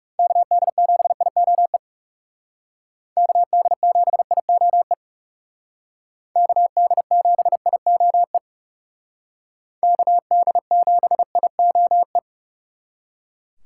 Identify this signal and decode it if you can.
Morse code, heard as a keyed tone, about 37 wpm, 700 Hz. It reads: KD7IOE KD7IOE KD7IOE KD7IOE
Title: Speed Racing - USA 2x3 Call Signs 30wpm